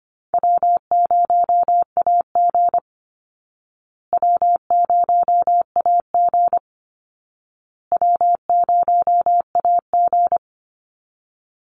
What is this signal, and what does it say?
Morse code, heard as a keyed tone, about 25 wpm, 700 Hz. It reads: W0AG W0AG W0AG
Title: Call Signs Repeated 2x - 25wpm